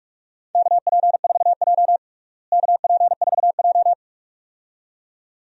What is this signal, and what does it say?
Morse code, heard as a keyed tone, about 44 wpm, 700 Hz. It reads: KP4J KP4J
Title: DXCC-100 Call Signs - 4 Characters - Repeated 2x - 45wpm